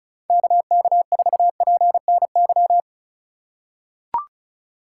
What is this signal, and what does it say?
Morse code, heard as a keyed tone, about 34 wpm, 690 Hz. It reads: KK4PNY
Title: USA Call Signs - 6 Characters - 6x Character Spacing - 35wpm